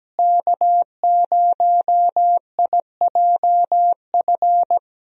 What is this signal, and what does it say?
Morse code, heard as a keyed tone, about 17 wpm, 700 Hz. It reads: K0IJF
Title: USA Call Signs - 5 Characters - Repeated 3x - 17wpm